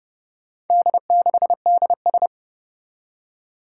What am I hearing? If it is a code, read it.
Morse code, heard as a keyed tone, about 30 wpm, 700 Hz. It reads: D6DS
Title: Four Challenging Letter-Number Combinations 30wpm